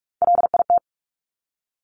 Morse code, heard as a keyed tone, about 46 wpm, 700 Hz. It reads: LIT
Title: Top Three Letter Combinations in Words 45wpm